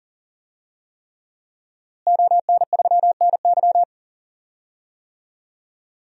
Morse code, heard as a keyed tone, about 40 wpm, 700 Hz. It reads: ON3NY
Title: DXCC-100 Call Signs - 5 Characters - Repeated 3x - 40wpm